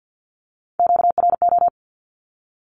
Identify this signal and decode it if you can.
Morse code, heard as a keyed tone, about 50 wpm, 700 Hz. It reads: QRO
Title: Q-Codes Repeated 3x 50wpm